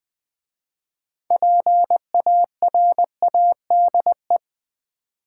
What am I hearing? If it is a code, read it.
Morse code, heard as a keyed tone, about 20 wpm, 700 Hz. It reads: PARADE
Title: 6 Letter Words - First 3 Letters Given - 20wpm